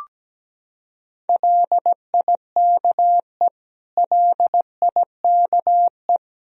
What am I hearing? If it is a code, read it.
Morse code, heard as a keyed tone, about 17 wpm, 700 Hz. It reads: LIKE LIKE
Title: Top 100 Words Repeated 17wpm